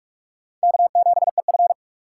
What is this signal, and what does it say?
Morse code, heard as a keyed tone, about 38 wpm, 700 Hz. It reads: K7EF